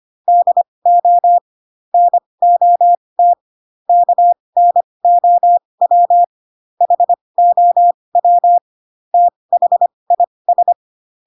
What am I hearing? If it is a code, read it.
Morse code, heard as a keyed tone, about 25 wpm, spaced as for 20 wpm, 700 Hz. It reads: DO NOT KNOW HOW THIS
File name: Pinocchio / Pinocchio ch1 15 25 ik1hge